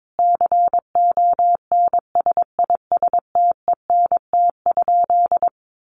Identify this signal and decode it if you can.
Morse code, heard as a keyed tone, about 22 wpm, 700 Hz. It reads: CONSISTENT?
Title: Sentences from Top 1500 Words 22wpm